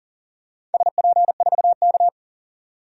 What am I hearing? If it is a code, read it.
Morse code, heard as a keyed tone, about 40 wpm, 700 Hz. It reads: SP4K
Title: DXCC-100 Call Signs - 4 Characters - 40wpm